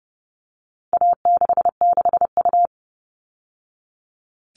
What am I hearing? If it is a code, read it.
Morse code, heard as a keyed tone, about 30 wpm, 700 Hz. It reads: A66U
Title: All Cut Numbers - 4-Digit 30wpm